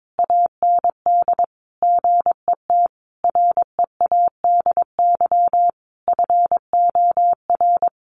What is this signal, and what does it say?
Morse code, heard as a keyed tone, about 22 wpm, 700 Hz. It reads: AND GET READY FOR
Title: Sentences from Top 800 Words 22wpm